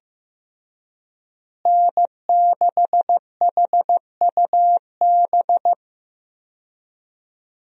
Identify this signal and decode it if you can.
Morse code, heard as a keyed tone, about 16 wpm, 700 Hz. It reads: N6HUB
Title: USA Call Signs - 5 Characters - Repeated 3x - 15wpm